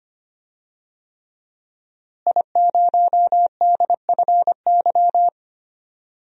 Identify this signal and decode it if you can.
Morse code, heard as a keyed tone, about 25 wpm, 700 Hz. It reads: I0DFY
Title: DXCC-100 Call Signs - 5 Characters - Repeated 3x - 25wpm